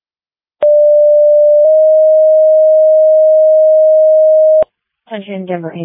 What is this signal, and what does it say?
Two-tone page, heard as a one-second tone, then a three-second tone, about 590 then 620 Hz, with speech. It is heard from 0.6 to 4.6 s.